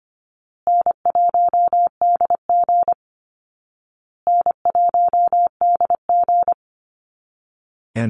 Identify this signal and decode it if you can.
Morse code, heard as a keyed tone, about 25 wpm, 700 Hz. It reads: N1DG N1DG
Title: Call Signs Repeated 2x - 25wpm